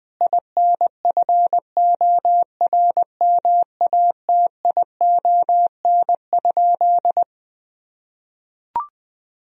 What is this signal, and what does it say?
Morse code, heard as a keyed tone, about 20 wpm, 700 Hz. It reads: INFORMATION?
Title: Sentences from Top 1000 Words - Review Part 3 - 20wpm